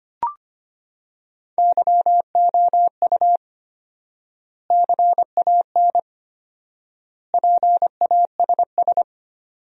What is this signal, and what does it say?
Morse code, heard as a keyed tone, about 25 wpm, 700 Hz. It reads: YOU CAN PASS
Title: 4x Word-Spacing - Sentences from Top 400 Words - 25wpm